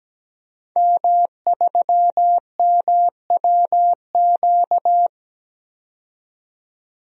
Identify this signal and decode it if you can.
Morse code, heard as a keyed tone, about 17 wpm, 700 Hz. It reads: M3MWQ
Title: DXCC-100 Call Signs - 5 Characters - 17wpm